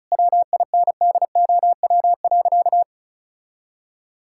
Morse code, heard as a keyed tone, about 36 wpm, 700 Hz. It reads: WINDOW.
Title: Sentences from Top 1600 Words 35wpm